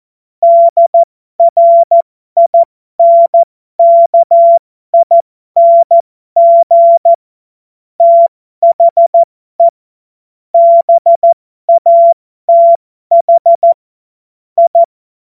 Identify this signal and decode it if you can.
Morse code, heard as a keyed tone, about 14 wpm, spaced as for 12 wpm, 680 Hz. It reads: DRINKING THE BATH I
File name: Morsekurs / Witze